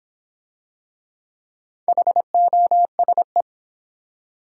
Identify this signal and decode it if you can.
Morse code, heard as a keyed tone, about 26 wpm, 700 Hz. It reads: HOSE